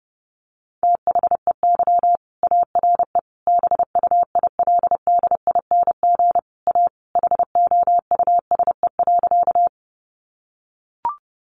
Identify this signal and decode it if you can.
Morse code, heard as a keyed tone, about 30 wpm, 700 Hz. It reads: THEY ARE BUILDING A HOUSE.
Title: Sentences from Top 700 Words 30wpm